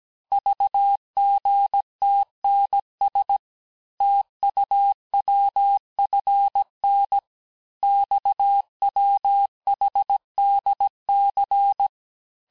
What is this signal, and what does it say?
Morse code, heard as a keyed tone, about 17 wpm, 800 Hz. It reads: VGTNS TUWFN XWHDC
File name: cours-du-lundi files / cours-1587-texte-chiffre-vitesse-16MPM